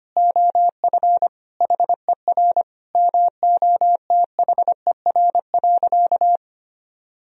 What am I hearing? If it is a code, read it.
Morse code, heard as a keyed tone, about 25 wpm, 700 Hz. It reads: OF HER MOTHER.
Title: Sentences from Top 1000 Words - Review Part 4 - 25wpm